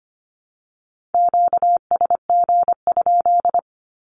Morse code, heard as a keyed tone, about 25 wpm, 700 Hz. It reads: QSG?
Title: Q-Codes Repeated 3x 25wpm